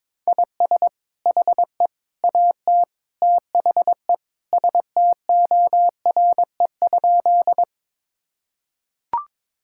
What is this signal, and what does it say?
Morse code, heard as a keyed tone, about 22 wpm, 700 Hz. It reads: IS HE AT THE STORE?
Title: Sentences from Top 1000 Words - Review Part 5 - 22wpm